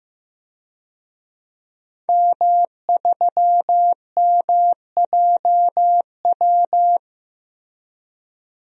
Morse code, heard as a keyed tone, about 15 wpm, 700 Hz. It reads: M3MJW
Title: DXCC-100 Call Signs - 5 Characters - 15wpm